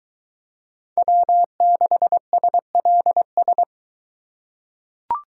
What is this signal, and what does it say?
Morse code, heard as a keyed tone, about 22 wpm, 700 Hz. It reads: W6SLS